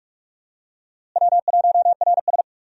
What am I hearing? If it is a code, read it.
Morse code, heard as a keyed tone, about 42 wpm, 700 Hz. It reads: W1RS
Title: USA Call Signs - 4 Characters - Repeated 3x - 45wpm